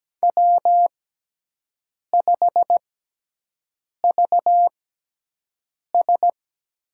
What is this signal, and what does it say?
Morse code, heard as a keyed tone, about 17 wpm, spaced as for 6 wpm, 700 Hz. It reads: W5VS
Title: USA Call Signs - 6x Character Spacing - 17wpm